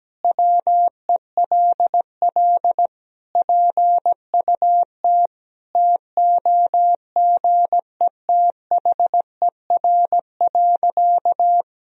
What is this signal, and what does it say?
Morse code, heard as a keyed tone, about 17 wpm, 700 Hz. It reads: WELL PUT TOGETHER.
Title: Sentences from Top 1000 Words - Review Part 3 - 17wpm